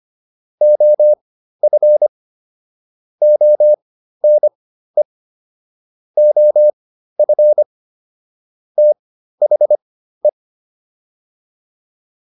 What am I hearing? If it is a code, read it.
Morse code, heard as a keyed tone, about 25 wpm, spaced as for 13 wpm, 600 Hz. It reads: OF ONE OF THE
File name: WD302 13